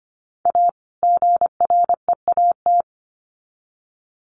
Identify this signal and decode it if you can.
Morse code, heard as a keyed tone, about 25 wpm, 700 Hz. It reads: A GREAT
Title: Sets of 2 Words 25wpm